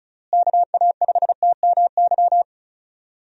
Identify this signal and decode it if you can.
Morse code, heard as a keyed tone, about 36 wpm, 700 Hz. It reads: KA5TMY
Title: USA Call Signs - 6 Characters - Repeated 3x - 35wpm